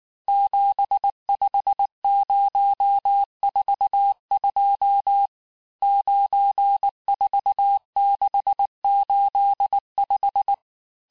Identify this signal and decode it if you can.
Morse code, heard as a keyed tone, about 19 wpm, 800 Hz. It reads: 75042 94685
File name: cours-du-lundi files / cours-1589-texte-chiffre-vitesse-18mpm